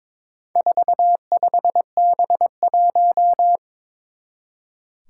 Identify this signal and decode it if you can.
Morse code, heard as a keyed tone, about 22 wpm, 700 Hz. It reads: 45B1
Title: Four Challenging Letter-Number Combinations 22wpm